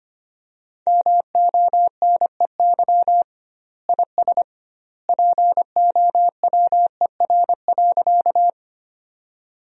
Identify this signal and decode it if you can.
Morse code, heard as a keyed tone, about 25 wpm, 700 Hz. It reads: MONEY IS POWER.
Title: 2x Word-Spacing - Sentences from Top 500 Words - 25wpm